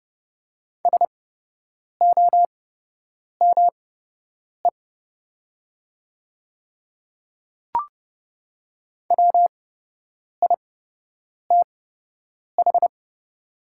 Morse code, heard as a keyed tone, about 29 wpm, spaced as for 8 wpm, 700 Hz. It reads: SOME WITH
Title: 8x Character Spacing - Top 100 Words - 30wpm